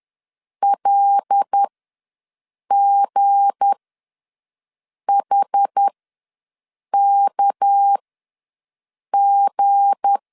Morse code, heard as a keyed tone, about 10 wpm, spaced as for 5 wpm, 800 Hz. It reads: LGHKG
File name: LEZIONI / 7 GF9Z / 7 G ws 10wpm 3843